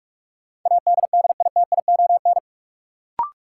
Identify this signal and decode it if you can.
Morse code, heard as a keyed tone, about 44 wpm, 700 Hz. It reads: ADDITION
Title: Sentences from Top 1500 Words - New Word Drill Alt - 45wpm